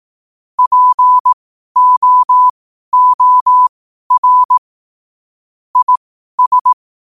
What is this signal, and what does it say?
Morse code, heard as a keyed tone, about 18 wpm, spaced as for 12 wpm, 1000 Hz. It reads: POOR IS
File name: 2026-01-07-12wpm